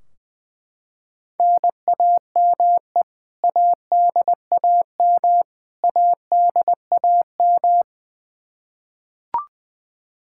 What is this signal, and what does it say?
Morse code, heard as a keyed tone, about 21 wpm, 700 Hz. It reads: NAME ADAM ADAM
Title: K1USN SST QSO Practice 20wpm